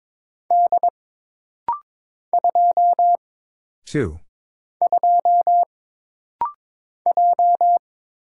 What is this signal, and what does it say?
Morse code, heard as a keyed tone, about 22 wpm, 700 Hz. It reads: D 2 2 J
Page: This rapid-fire timing is geared to push your ICR (Instant Character Recognition) skills.